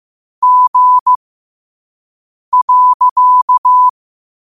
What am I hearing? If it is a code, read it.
Morse code, heard as a keyed tone, about 15 wpm, spaced as for 5 wpm, 1000 Hz. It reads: G.
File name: qotdfiles / 5wpm